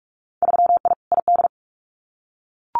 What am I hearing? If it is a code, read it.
Morse code, heard as a keyed tone, about 44 wpm, 700 Hz. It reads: 3I ID
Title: Winter Field Day Exchanges 45wpm